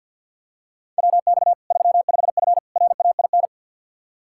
Morse code, heard as a keyed tone, about 49 wpm, 700 Hz. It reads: WX 35F RAIN
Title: K1USN SST QSO Practice 50wpm